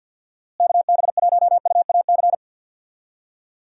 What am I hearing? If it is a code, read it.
Morse code, heard as a keyed tone, about 50 wpm, 700 Hz. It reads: KB1UAC
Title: USA Call Signs - 6 Characters - Repeated 3x - 50wpm